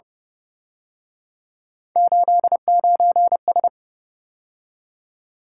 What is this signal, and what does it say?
Morse code, heard as a keyed tone, about 30 wpm, 700 Hz. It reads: 89S
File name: Speed-Racing - Three Letter-Number Combinations - 25wpm